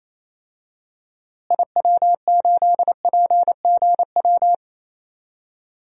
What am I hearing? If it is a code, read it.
Morse code, heard as a keyed tone, about 28 wpm, 700 Hz. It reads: IW8PGW
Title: DXCC-100 Call Signs - 6 Characters - Repeated 2x - 28wpm